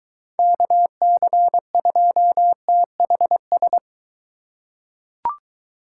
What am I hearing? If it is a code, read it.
Morse code, heard as a keyed tone, about 23 wpm, 700 Hz. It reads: KC2THS